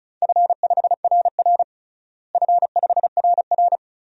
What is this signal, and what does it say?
Morse code, heard as a keyed tone, about 36 wpm, 700 Hz. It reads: F5RR F5RR
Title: DXCC-100 Call Signs - 4 Characters - Repeated 2x - 35wpm